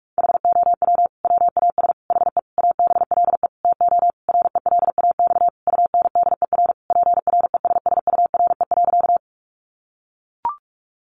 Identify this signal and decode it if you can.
Morse code, heard as a keyed tone, about 44 wpm, 700 Hz. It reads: HOW WAS HE ABLE TO RELAX UNDER PRESSURE.
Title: Sentences from Top 1400 Words 45wpm